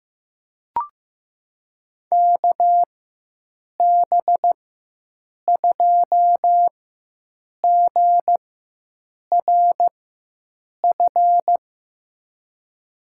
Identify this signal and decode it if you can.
Morse code, heard as a keyed tone, about 15 wpm, spaced as for 7 wpm, 700 Hz. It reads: KB2GRF
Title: USA Call Signs - 6 Characters - 4x Character Spacing - 15wpm